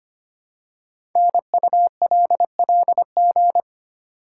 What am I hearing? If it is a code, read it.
Morse code, heard as a keyed tone, about 25 wpm, 700 Hz. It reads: NULLG
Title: Code Groups - 5 Letter:Numbers - 25wpm